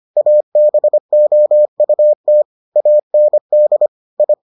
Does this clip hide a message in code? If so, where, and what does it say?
Yes: Morse code from 0.2 to 4.3 s, heard as a keyed tone, about 25 wpm, 600 Hz. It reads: ABOUT AND I